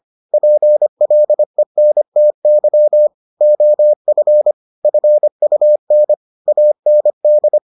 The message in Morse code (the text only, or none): PLENTY OF FUN AND